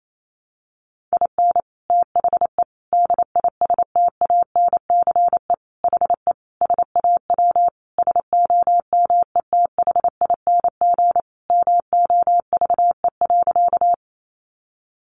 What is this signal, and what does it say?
Morse code, heard as a keyed tone, about 28 wpm, 700 Hz. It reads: IN THE DISTANCE HE SAW SOMETHING MOVE.
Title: Sentences from Top 1200 Words 28wpm